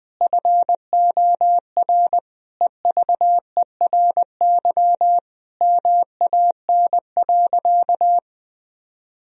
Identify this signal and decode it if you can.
Morse code, heard as a keyed tone, about 20 wpm, 700 Hz. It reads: FOR EVERY MAN.
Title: Sentences from Top 1000 Words - Review Part 1 - 20wpm